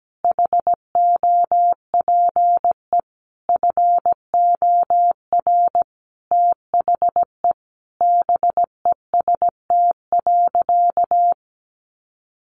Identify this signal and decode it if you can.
Morse code, heard as a keyed tone, about 17 wpm, 700 Hz. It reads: HOPE FOR THE BEST.
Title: Sentences from Top 1000 Words 17wpm